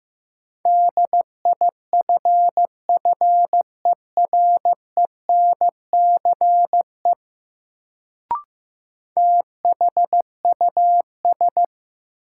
Morse code, heard as a keyed tone, about 15 wpm, 700 Hz. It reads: DIFFERENCE THUS
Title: Sentences from Top 800 Words - New Word Drill - 15wpm